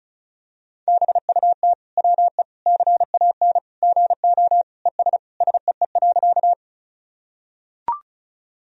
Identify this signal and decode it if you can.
Morse code, heard as a keyed tone, about 36 wpm, 700 Hz. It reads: BUT WE CAN GO ES SEE.
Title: Sentences from Top 100 Words and Abbreviations 35wpm